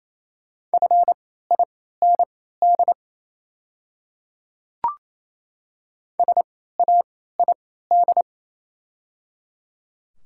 Morse code, heard as a keyed tone, about 28 wpm, spaced as for 16 wpm, 700 Hz. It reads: FIND SAID
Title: 3x Character Spacing - Top 100 Words - 28wpm